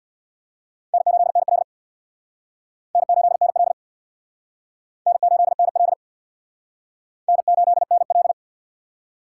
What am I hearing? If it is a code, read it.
Morse code, heard as a keyed tone, about 62 wpm, 700 Hz. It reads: N8NL N8NL N8NL N8NL
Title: Speed Racing - USA 1x2 Call Signs 50wpm